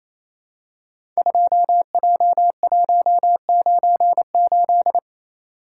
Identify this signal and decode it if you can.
Morse code, heard as a keyed tone, about 28 wpm, 700 Hz. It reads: 2J198
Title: Code Groups - 5 Letter:Numbers - 28wpm